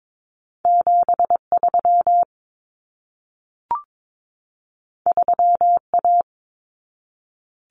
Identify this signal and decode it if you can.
Morse code, heard as a keyed tone, about 22 wpm, 700 Hz. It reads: 73 3A
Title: Common Cut Numbers - 2-Digit 22wpm